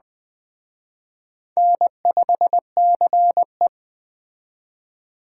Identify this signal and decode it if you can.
Morse code, heard as a keyed tone, about 21 wpm, 700 Hz. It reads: N5CE